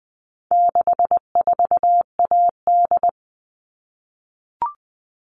Morse code, heard as a keyed tone, about 20 wpm, 700 Hz. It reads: 64AD
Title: All Cut Numbers - 4-Digit 20wpm